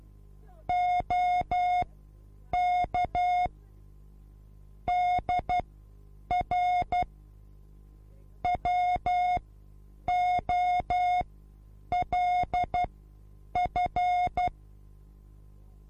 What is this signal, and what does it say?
Morse code, heard as a keyed tone, about 12 wpm, spaced as for 8 wpm, 720 Hz. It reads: OK DR WOLF
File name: repos / Morsekurs / DJ4UF Morsen24